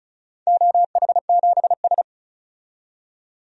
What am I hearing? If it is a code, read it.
Morse code, heard as a keyed tone, about 36 wpm, 700 Hz. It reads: OH7S
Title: DXCC-100 Call Signs - 4 Characters - 35wpm